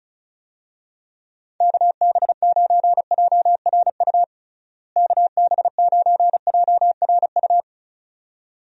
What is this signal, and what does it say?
Morse code, heard as a keyed tone, about 36 wpm, 700 Hz. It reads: KB9JRU KB9JRU
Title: USA Call Signs - 6 Characters - Repeated 2x - 35wpm